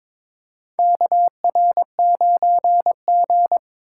Morse code, heard as a keyed tone, about 22 wpm, 700 Hz. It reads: KR9G
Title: USA Call Signs - 4 Characters - 4x Character Spacing - 22wpm